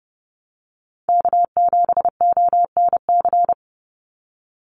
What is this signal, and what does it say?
Morse code, heard as a keyed tone, about 29 wpm, 700 Hz. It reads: K7ONC
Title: USA Call Signs 30wpm